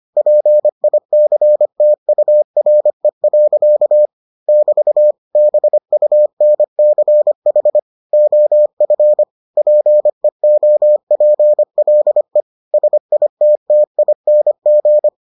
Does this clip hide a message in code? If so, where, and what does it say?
Yes: Morse code from 0.2 to 15.1 s, heard as a keyed tone, about 25 wpm, 600 Hz. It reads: PICTURE. <BT> BUNCH OF PEOPLE SITTING